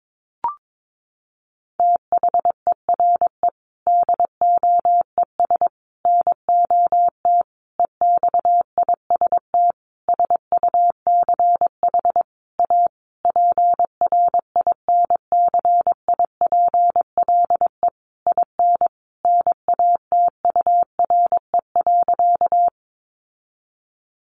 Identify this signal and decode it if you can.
Morse code, heard as a keyed tone, about 22 wpm, 700 Hz. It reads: THERE DOES NOT EXIST SUCH A PRINCIPLE IN NATURE.
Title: Sentences from Top 1400 Words 22wpm